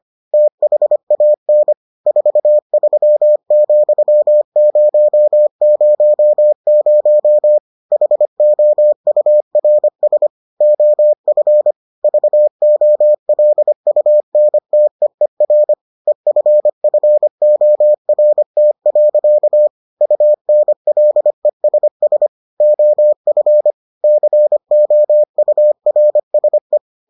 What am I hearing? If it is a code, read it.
Morse code, heard as a keyed tone, about 25 wpm, 600 Hz. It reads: THAN 43,000 HOURS OF VOLUNTEER EFFORT. UNLESS OF COURSE